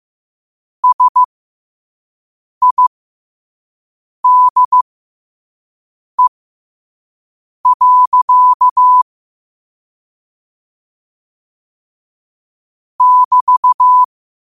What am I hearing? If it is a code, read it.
Morse code, heard as a keyed tone, about 15 wpm, spaced as for 5 wpm, 1000 Hz. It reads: SIDE. <BT>
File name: qotdfiles / 5wpm